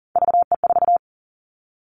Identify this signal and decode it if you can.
Morse code, heard as a keyed tone, about 40 wpm, 700 Hz. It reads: VE4
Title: Callsign Prefixes for Top 25 DXCC - 40wpm